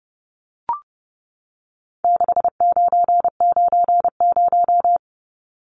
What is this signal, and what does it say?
Morse code, heard as a keyed tone, about 30 wpm, 700 Hz. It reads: 6990
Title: Numbers - 4-Digit 30wpm